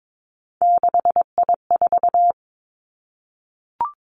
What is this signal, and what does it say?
Morse code, heard as a keyed tone, about 22 wpm, 700 Hz. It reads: 6I4
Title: Three Challenging Letter-Number Combinations 22wpm